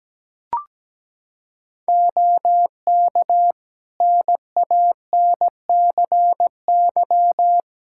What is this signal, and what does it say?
Morse code, heard as a keyed tone, about 17 wpm, 700 Hz. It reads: OK NANCY
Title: Practice QSO - Exchange 4 - Ending - 17wpm